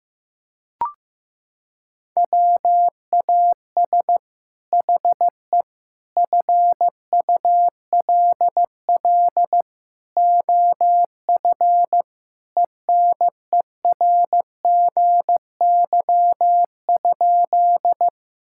Morse code, heard as a keyed tone, about 15 wpm, 700 Hz. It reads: WAS HE FULL OF ENERGY?
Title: Sentences from Top 1000 Words - Review Part 6 - 15wpm